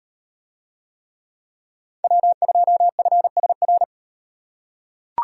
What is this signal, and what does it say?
Morse code, heard as a keyed tone, about 38 wpm, 700 Hz. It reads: W2FSR